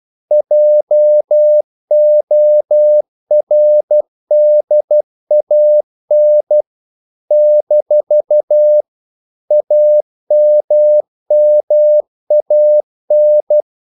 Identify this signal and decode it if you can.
Morse code, heard as a keyed tone, about 12 wpm, 600 Hz. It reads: JORDAN - AMMAN
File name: CW / Practice / NationalCapitals12